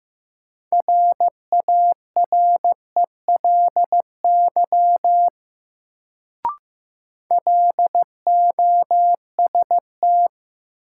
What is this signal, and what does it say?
Morse code, heard as a keyed tone, about 15 wpm, 700 Hz. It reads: RARELY LOST
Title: Sentences from Top 1400 Words - New Word Drill - 15wpm